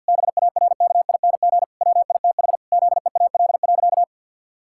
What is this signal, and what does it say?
Morse code, heard as a keyed tone, about 49 wpm, 700 Hz. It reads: BARKING WITH ZEAL.
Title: Pangram Challenge v2 - 50wpm